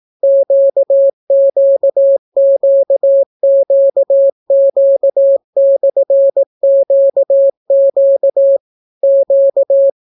Morse code, heard as a keyed tone, about 18 wpm, 550 Hz. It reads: QQQQQ/QQ Q